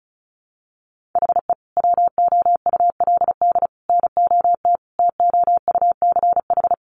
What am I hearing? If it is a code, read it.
Morse code, heard as a keyed tone, about 34 wpm, 700 Hz. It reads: HE WOULD NOT TOUCH
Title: Sentences from Top 1200 Words 35wpm